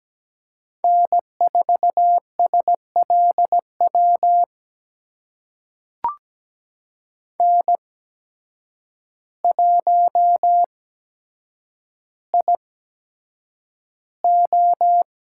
Morse code, heard as a keyed tone, about 17 wpm, spaced as for 8 wpm, 700 Hz. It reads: N4SLW N1IO
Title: USA Call Signs - 5 Characters - 8x Character Spacing - 17wpm